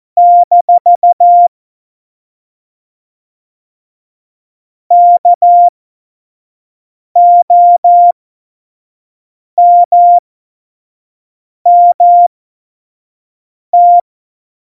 Morse code, heard as a keyed tone, about 14 wpm, spaced as for 5 wpm, 700 Hz. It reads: - KOMMT
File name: Morsekurs / Witze / Witz 18. September 2018, 25 F70